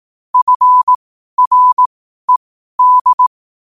Morse code, heard as a keyed tone, about 18 wpm, spaced as for 12 wpm, 1000 Hz. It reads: FRED
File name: qotdfiles / 12wpm